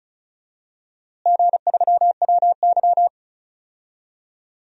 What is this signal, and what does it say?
Morse code, heard as a keyed tone, about 36 wpm, 700 Hz. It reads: G3WY
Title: DXCC-100 Call Signs - 4 Characters - Repeated 3x - 35wpm